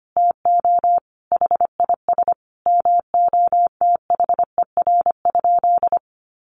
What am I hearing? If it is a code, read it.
Morse code, heard as a keyed tone, about 25 wpm, 700 Hz. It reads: TO HIS MOTHER?
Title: Sentences from Top 1300 Words 25wpm